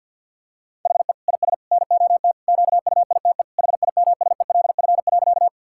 Morse code, heard as a keyed tone, about 50 wpm, 700 Hz. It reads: HE IS NOT QUITE HIMSELF.
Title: Sentences from Top 1600 Words 50wpm